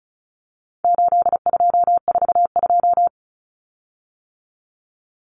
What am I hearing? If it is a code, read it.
Morse code, heard as a keyed tone, about 34 wpm, 700 Hz. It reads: 8242
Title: Common Cut Numbers - 4-Digit 35wpm